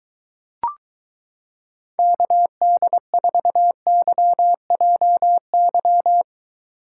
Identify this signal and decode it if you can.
Morse code, heard as a keyed tone, about 22 wpm, 700 Hz. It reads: KD4YJY